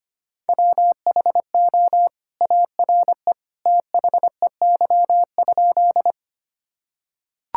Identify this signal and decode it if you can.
Morse code, heard as a keyed tone, about 25 wpm, 700 Hz. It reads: WHO ARE THEY?
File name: practice / 25wpm / Sentences with 2-Word Phrases - 25wpm